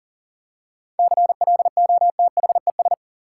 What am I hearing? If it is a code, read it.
Morse code, heard as a keyed tone, about 40 wpm, 700 Hz. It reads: CLOTHES
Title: Seven Letter Words - 40wpm